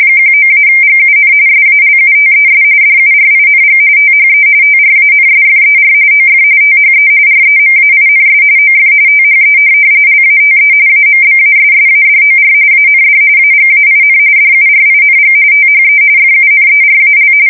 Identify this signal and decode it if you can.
RTTY, about 45.45 baud. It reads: 19-GUN SALUTES DURING THE TRANSFER OF THE REMAINS OF THE LATE AMBASSADOR AND AGAIN DURING THE FUNERAL
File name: fsk170 / USS-Missouri